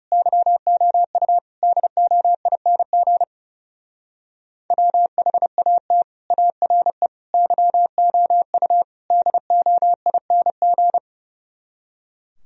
Morse code, heard as a keyed tone, about 38 wpm, 700 Hz. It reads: YOU DOING WHAT ARE YOU DOING